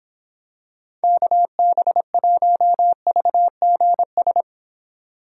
Morse code, heard as a keyed tone, about 26 wpm, 700 Hz. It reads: KB1VGS